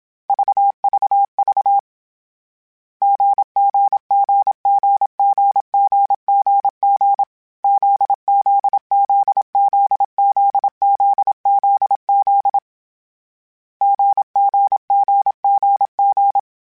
Morse code, heard as a keyed tone, about 27 wpm, 800 Hz. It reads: VVV GGGGGGGG ZZZZZZZZ GGGGG
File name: Short Runs / Short Runs-mp3s / GZ GZ7 @25